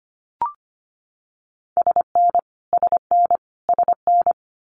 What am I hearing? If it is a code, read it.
Morse code, heard as a keyed tone, about 25 wpm, 700 Hz. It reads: SN SN SN
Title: Multi-letter Phonemes Repeated 3x 25wpm